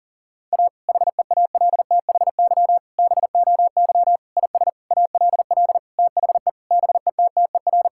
Morse code, heard as a keyed tone, about 37 wpm, 700 Hz. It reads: A HEALTHY BOY IS ALL THE BETTER
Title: Sentences from Top 1000 Words - Review Part 5 - 40wpm